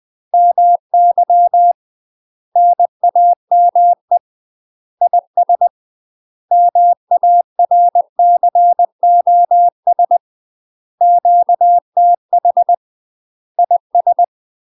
Morse code, heard as a keyed tone, about 20 wpm, 700 Hz. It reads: MY NAME IS MARCOS QTH IS
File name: UK MASTER QSOs TRAINING / 20WPM / UK MASTER QSOs 0043